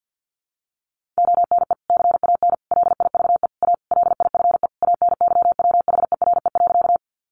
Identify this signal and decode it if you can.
Morse code, heard as a keyed tone, about 50 wpm, 700 Hz. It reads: ONE CAN LIVE A LIFE ANYWHERE.
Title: Sentences from Top 1400 Words 50wpm